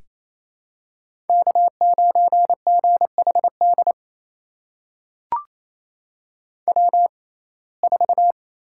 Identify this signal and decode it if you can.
Morse code, heard as a keyed tone, about 28 wpm, spaced as for 20 wpm, 700 Hz. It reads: K9GHD W4
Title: USA Call Signs - 5 Characters - 6x Character Spacing - 28wpm